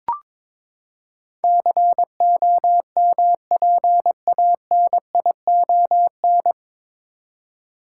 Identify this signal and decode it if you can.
Morse code, heard as a keyed tone, about 22 wpm, 700 Hz. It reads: COMPANION
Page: Morse Code Ninja practice 22wpm